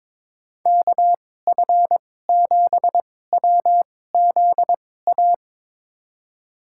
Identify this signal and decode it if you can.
Morse code, heard as a keyed tone, about 22 wpm, spaced as for 16 wpm, 700 Hz. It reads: KF7WZA
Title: USA Call Signs - 6 Characters - 2x Character Spacing - 22wpm